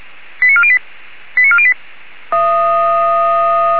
Five-tone selective call (ZVEI), heard as quick runs of five tones, tones of about 70 ms. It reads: 89408 89408
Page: ZVEI